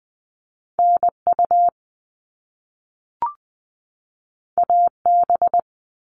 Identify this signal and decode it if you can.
Morse code, heard as a keyed tone, about 19 wpm, 700 Hz. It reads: NU AB
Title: Top Two Letter Combinations in Words 20wpm